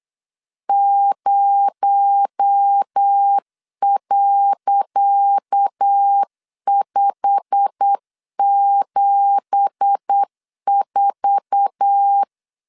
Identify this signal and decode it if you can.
Morse code, heard as a keyed tone, about 8 wpm, 800 Hz. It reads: 0.574
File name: LEZIONI / 10 SIGNS / 10C DOT 8wpm 3649